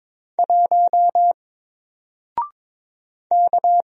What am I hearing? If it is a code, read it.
Morse code, heard as a keyed tone, about 22 wpm, 700 Hz. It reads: 1 K
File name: Single-Letter-Number-22wpm